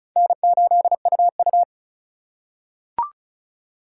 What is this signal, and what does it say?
Morse code, heard as a keyed tone, about 34 wpm, 700 Hz. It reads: N8UU
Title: USA Call Signs - 4 Characters - 4x Character Spacing - 35wpm